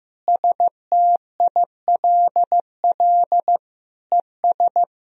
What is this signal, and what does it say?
Morse code, heard as a keyed tone, about 16 wpm, 700 Hz. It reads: STILL ES
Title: Sentences from Top 300 Words and Abbreviations 15wpm